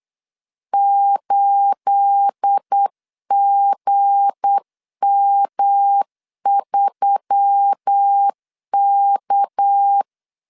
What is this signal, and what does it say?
Morse code, heard as a keyed tone, about 8 wpm, 800 Hz. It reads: 8GM3K
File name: 8wpm